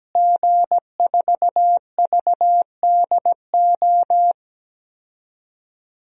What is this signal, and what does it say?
Morse code, heard as a keyed tone, about 17 wpm, 700 Hz. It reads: G4VDO
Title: DXCC-100 Call Signs - 5 Characters - Repeated 3x - 17wpm